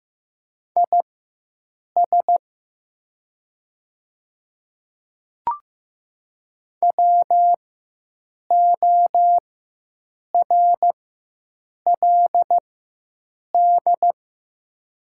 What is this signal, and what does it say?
Morse code, heard as a keyed tone, about 15 wpm, spaced as for 7 wpm, 700 Hz. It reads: IS WORLD
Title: 4x Character Spacing - Top 200 Words - 15wpm